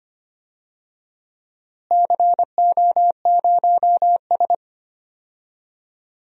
Morse code, heard as a keyed tone, about 25 wpm, 700 Hz. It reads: CO0S
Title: DXCC-100 Call Signs - 4 Characters - Repeated 3x - 25wpm